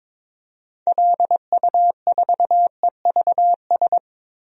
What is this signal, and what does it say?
Morse code, heard as a keyed tone, about 22 wpm, 700 Hz. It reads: LU4EVS
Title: DXCC-100 Call Signs - 6 Characters - 22wpm